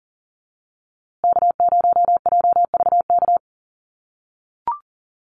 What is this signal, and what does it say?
Morse code, heard as a keyed tone, about 40 wpm, 700 Hz. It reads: K0JVK
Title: USA Call Signs Repeated 3x 40wpm